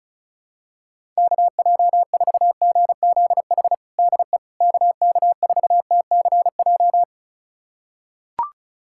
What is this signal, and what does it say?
Morse code, heard as a keyed tone, about 34 wpm, 700 Hz. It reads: KJ4GZH DE KK4TCJ
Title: USA Callsign DE Callsign - 6 Characters - 35wpm